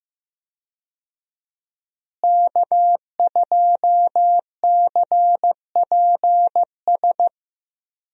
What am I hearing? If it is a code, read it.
Morse code, heard as a keyed tone, about 15 wpm, 700 Hz. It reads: K2CPS
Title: USA Call Signs - 5 Characters - Repeated 3x - 15wpm